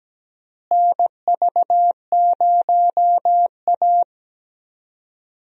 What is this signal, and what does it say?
Morse code, heard as a keyed tone, about 17 wpm, 700 Hz. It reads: NV0A
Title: USA Call Signs - 4 Characters - Repeated 2x - 17wpm